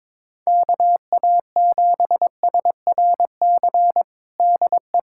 Morse code, heard as a keyed tone, about 22 wpm, 700 Hz. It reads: KA7SRC DE
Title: USA Callsign DE Callsign - 6 Characters - 22wpm